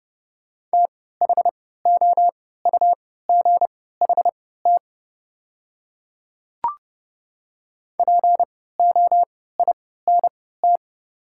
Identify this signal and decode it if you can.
Morse code, heard as a keyed tone, about 30 wpm, spaced as for 17 wpm, 690 Hz. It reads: THOUGHT POINT
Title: 3x Character Spacing - Top 300 Words - 30wpm